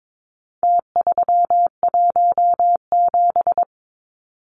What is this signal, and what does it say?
Morse code, heard as a keyed tone, about 22 wpm, 700 Hz. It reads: T317
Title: Numbers - Contest Style - 4-Digit 22wpm